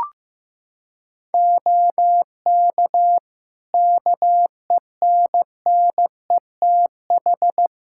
Morse code, heard as a keyed tone, about 15 wpm, 700 Hz. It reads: OK KENNETH
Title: Practice QSO - Exchange 3 - More Info - 15wpm